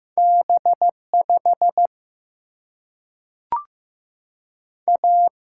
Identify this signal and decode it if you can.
Morse code, heard as a keyed tone, about 15 wpm, 700 Hz. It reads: B 5 A
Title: Single-Letter Single-Number 15wpm